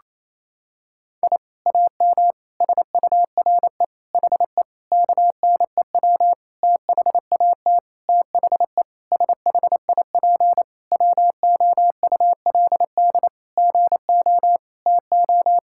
Morse code, heard as a keyed tone, about 28 wpm, 700 Hz. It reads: I AM SURE HE KNEW THAT THE SHIP WOULD GO TO
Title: Sentences from Top 1000 Words - Review Part 6 - 28wpm